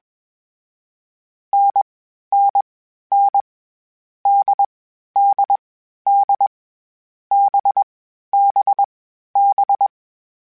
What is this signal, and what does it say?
Morse code, heard as a keyed tone, about 21 wpm, 800 Hz. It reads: N N N D D D B B B
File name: Variable Speed / Variable Speed-mp3s / GZ GZ7 sp @20